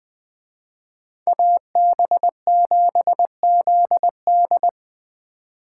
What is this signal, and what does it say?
Morse code, heard as a keyed tone, about 20 wpm, 700 Hz. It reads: AB7ZD
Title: Speed Racing - USA 2x2 Call Signs 17wpm